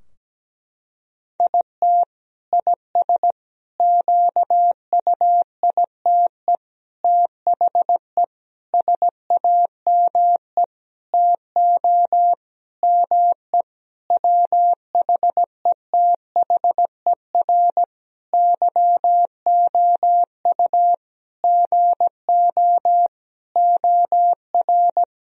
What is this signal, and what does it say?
Morse code, heard as a keyed tone, about 17 wpm, 700 Hz. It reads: IT IS QUITE THE SAME TO ME WHETHER YOU GO OR
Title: Sentences from Top 1000 Words - Review Part 3 - 17wpm